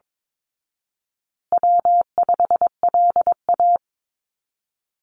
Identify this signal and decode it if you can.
Morse code, heard as a keyed tone, about 22 wpm, 700 Hz. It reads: W5LA
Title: Call Signs Repeated 2x - 22wpm